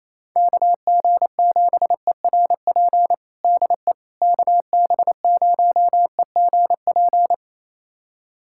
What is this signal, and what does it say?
Morse code, heard as a keyed tone, about 28 wpm, 700 Hz. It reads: KG7ERP DE KB0EGP
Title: USA Callsign DE Callsign 28wpm